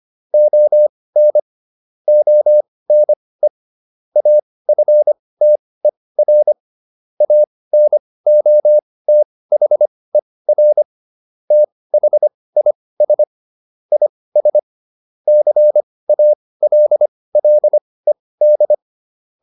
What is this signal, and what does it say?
Morse code, heard as a keyed tone, about 25 wpm, spaced as for 18 wpm, 600 Hz. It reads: ON ONE AFTER ANOTHER THIS IS CALLED